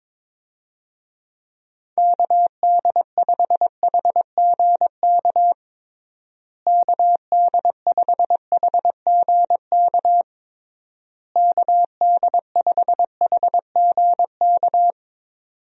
Morse code, heard as a keyed tone, about 23 wpm, 700 Hz. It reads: KD5HGK KD5HGK KD5HGK
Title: USA Call Signs - 6 Characters - Repeated 3x - 22wpm